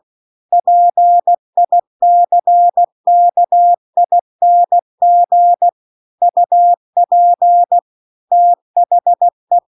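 Morse code, heard as a keyed tone, about 16 wpm, 700 Hz. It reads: PICKING UP THE
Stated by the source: Morse practice text at 16 WPM.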